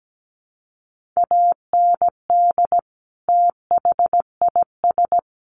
Morse code, heard as a keyed tone, about 17 wpm, 700 Hz. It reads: AND THIS
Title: Sentences from Top 300 Words 17wpm